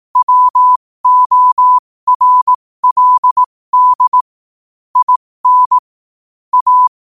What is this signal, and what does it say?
Morse code, heard as a keyed tone, about 18 wpm, spaced as for 15 wpm, 1000 Hz. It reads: WORLD IN A
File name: qotdfiles / 15wpm